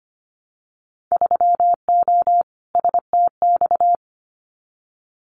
Morse code, heard as a keyed tone, about 25 wpm, 690 Hz. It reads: 3O STX
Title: Winter Field Day Exchanges 25wpm